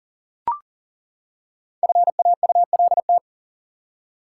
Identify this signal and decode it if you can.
Morse code, heard as a keyed tone, about 39 wpm, 700 Hz. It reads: FAULT
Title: Five Letter Words - 40wpm